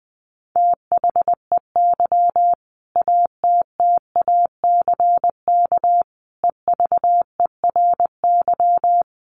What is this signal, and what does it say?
Morse code, heard as a keyed tone, about 20 wpm, 700 Hz. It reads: THEY ATTACK EVERY
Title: Sentences from Top 1500 Words 20wpm